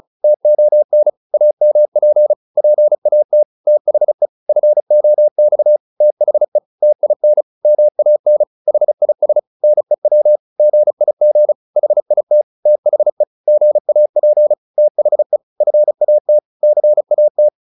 Morse code, heard as a keyed tone, about 35 wpm, 600 Hz. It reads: TON AMP PAT THE FOX THE TIN MAN HIS NEW GIG HIT THE GAP THE FAT CAT